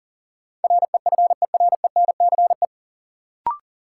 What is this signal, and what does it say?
Morse code, heard as a keyed tone, about 40 wpm, 700 Hz. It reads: REFERENCE
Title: Sentences from Top 1200 Words - New Word Drill Alt - 40wpm